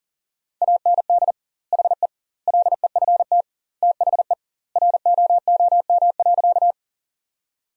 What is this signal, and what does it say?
Morse code, heard as a keyed tone, about 40 wpm, 700 Hz. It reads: AND HE LEFT THE ROOM.
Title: 2x Word-Spacing - Sentences from Top 300 Words - 40wpm